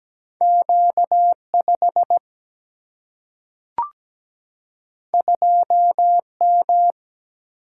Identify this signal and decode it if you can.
Morse code, heard as a keyed tone, about 17 wpm, 700 Hz. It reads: Q5 2M
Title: Single-Letter Single-Number 17wpm